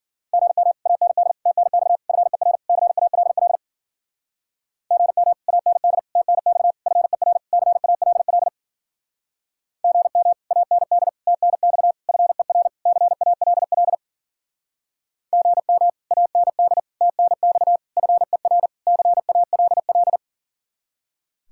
Morse code, heard as a keyed tone, about 49 wpm, 700 Hz. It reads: GM AND TNX FER CALL GM AND TNX FER CALL GM AND TNX FER CALL GM AND TNX FER CALL
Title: Speed-Racing Common QSO Phrases - 40wpm